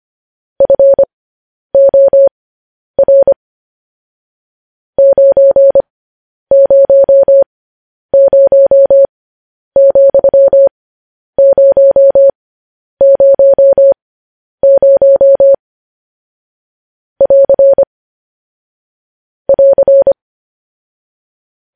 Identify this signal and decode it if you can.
Morse code, heard as a keyed tone, about 25 wpm, spaced as for 10 wpm, 550 Hz. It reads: FOR 900,000 <AR> <AR>